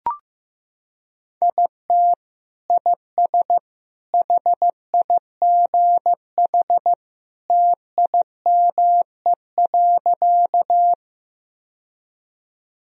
Morse code, heard as a keyed tone, about 15 wpm, 700 Hz. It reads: IT IS HIGH TIME.
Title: Sentences from Top 1000 Words - Review Part 1 - 15wpm